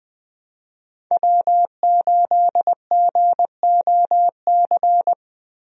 Morse code, heard as a keyed tone, about 20 wpm, 700 Hz. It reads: W8GOC
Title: USA Call Signs - 5 Characters - 20wpm